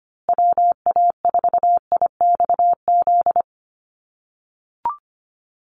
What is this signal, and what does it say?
Morse code, heard as a keyed tone, about 25 wpm, 700 Hz. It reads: WA4IXZ
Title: USA Call Signs Repeated 3x 25wpm